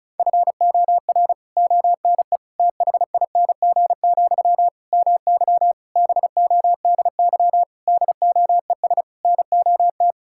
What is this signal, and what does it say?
Morse code, heard as a keyed tone, about 34 wpm, 700 Hz. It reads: FOR ONE THING, MY BODY DOES NOT
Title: Sentences from Top 1000 Words - Review Part 6 - 35wpm